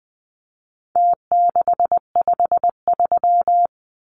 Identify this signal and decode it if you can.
Morse code, heard as a keyed tone, about 20 wpm, 700 Hz. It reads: T653
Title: Numbers - Contest Style - 4-Digit 20wpm